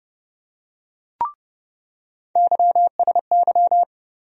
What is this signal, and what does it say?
Morse code, heard as a keyed tone, about 30 wpm, 700 Hz. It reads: YSY
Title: USA Call Sign Postfixes - 3 Letters 30wpm